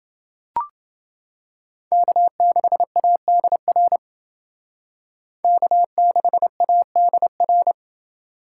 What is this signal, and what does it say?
Morse code, heard as a keyed tone, about 28 wpm, 700 Hz. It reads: K6ADR K6ADR